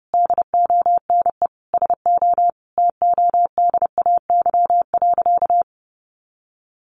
Morse code, heard as a keyed tone, about 30 wpm, 700 Hz. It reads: DONE SO TODAY.
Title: Sentences from Top 1500 Words 30wpm